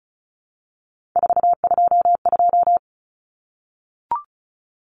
Morse code, heard as a keyed tone, about 34 wpm, 700 Hz. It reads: 422
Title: Common Cut Numbers - 3-Digit 35wpm